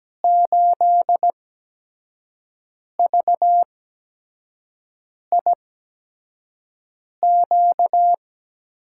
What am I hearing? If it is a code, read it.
Morse code, heard as a keyed tone, about 17 wpm, spaced as for 5 wpm, 700 Hz. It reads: 8VIQ
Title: USA Call Signs - 6 Characters - 8x Character Spacing - 17wpm